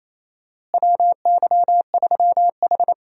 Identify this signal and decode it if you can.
Morse code, heard as a keyed tone, about 27 wpm, 700 Hz. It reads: WY3H
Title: USA Call Signs - 4 Characters - Repeated 2x - 28wpm